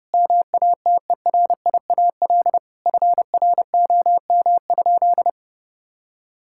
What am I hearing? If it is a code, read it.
Morse code, heard as a keyed tone, about 30 wpm, 700 Hz. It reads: MATERIAL FROM?
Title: Sentences from Top 1000 Words - Review Part 5 - 30wpm